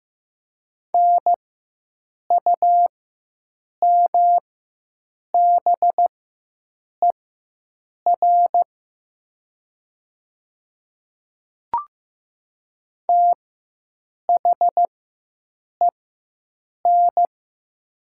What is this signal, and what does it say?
Morse code, heard as a keyed tone, about 15 wpm, spaced as for 7 wpm, 700 Hz. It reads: NUMBER THEN
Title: 4x Character Spacing - Top 200 Words - 15wpm